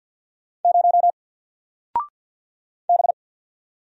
Morse code, heard as a keyed tone, about 51 wpm, 700 Hz. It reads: 0 B
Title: Single Character (Letter:Number:Punctuation:Prosign) Alt 50wpm